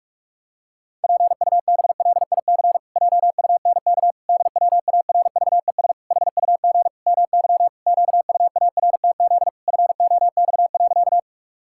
Morse code, heard as a keyed tone, about 46 wpm, 700 Hz. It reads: PUBLIC JUNK DWARVES HUG MY QUARTZ FOX.
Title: Pangram Challenge v2 - 45wpm